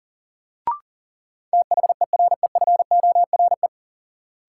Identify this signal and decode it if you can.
Morse code, heard as a keyed tone, about 40 wpm, 700 Hz. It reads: THEREFORE
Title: Sentences from Top 800 Words - New Word Drill - 40wpm